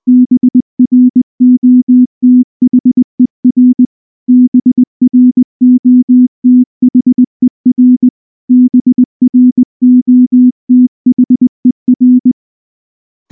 Morse code, heard as a keyed tone, about 19 wpm, 250 Hz. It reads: BROTHER BROTHER BROTHER
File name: morse / morse-training-20WPM-2